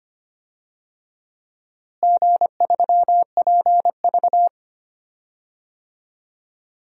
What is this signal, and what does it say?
Morse code, heard as a keyed tone, about 25 wpm, 700 Hz. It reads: G3PV
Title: DXCC-100 Call Signs - 4 Characters - Repeated 3x - 25wpm